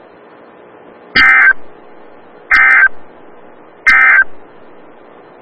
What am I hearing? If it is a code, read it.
Emergency Alert System (SAME) tones, heard as harsh buzzing bursts. It is heard from 1.0 to 4.3 s.
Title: Des Moines, IA NOAA Weather Radio - WXL57